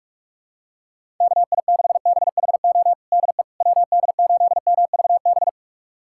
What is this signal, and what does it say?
Morse code, heard as a keyed tone, about 46 wpm, 700 Hz. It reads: KI6BHO DE WD8MVB
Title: USA Callsign DE Callsign 45wpm